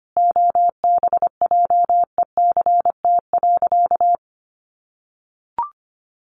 Morse code, heard as a keyed tone, about 25 wpm, 700 Hz. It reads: OBJECT.
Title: Sentences from Top 800 Words 25wpm